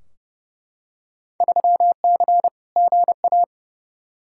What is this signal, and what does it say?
Morse code, heard as a keyed tone, about 30 wpm, 700 Hz. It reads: 3C GA
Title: Field Day Exchanges Encore - 30wpm